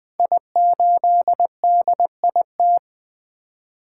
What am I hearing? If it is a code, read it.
Morse code, heard as a keyed tone, about 20 wpm, 700 Hz. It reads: I8DIT
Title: Code Groups - 5 Letter:Numbers - 20wpm